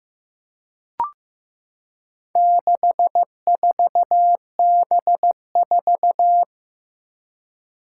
Morse code, heard as a keyed tone, about 16 wpm, 700 Hz. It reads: 64B4
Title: Four Challenging Letter-Number Combinations 15wpm